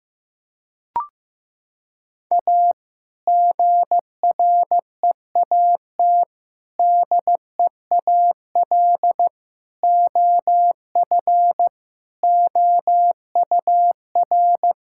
Morse code, heard as a keyed tone, about 15 wpm, 700 Hz. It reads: A GREAT DEAL OF OUR
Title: Sentences from Top 1000 Words - Review Part 6 - 15wpm